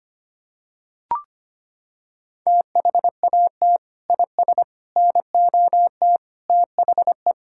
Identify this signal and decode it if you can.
Morse code, heard as a keyed tone, about 25 wpm, 700 Hz. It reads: THAT IS NOT THE
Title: Sentences from Top 1000 Words - Review Part 5 - 25wpm